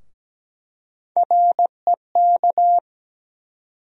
Morse code, heard as a keyed tone, about 17 wpm, 700 Hz. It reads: REK
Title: USA Call Sign Postfixes - 3 Letters 17wpm